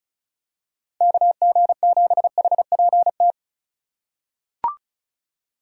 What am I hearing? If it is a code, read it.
Morse code, heard as a keyed tone, about 36 wpm, 700 Hz. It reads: KG7HPT
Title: USA Call Signs - 6 Characters - 3x Character Spacing - 35wpm